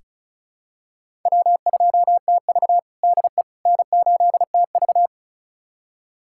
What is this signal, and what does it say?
Morse code, heard as a keyed tone, about 36 wpm, 700 Hz. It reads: W2TV DE N8TV
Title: USA Callsign DE Callsign - 4 Characters - 35wpm